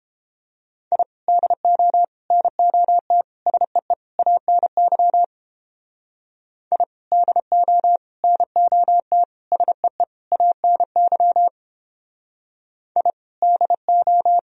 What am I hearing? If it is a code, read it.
Morse code, heard as a keyed tone, about 29 wpm, 700 Hz. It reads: I DO NOT SEE ANY I DO NOT SEE ANY I DO